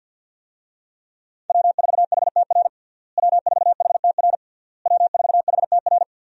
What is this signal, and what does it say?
Morse code, heard as a keyed tone, about 50 wpm, 700 Hz. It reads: W4HTR W4HTR W4HTR
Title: USA Call Signs - 5 Characters - Repeated 3x - 50wpm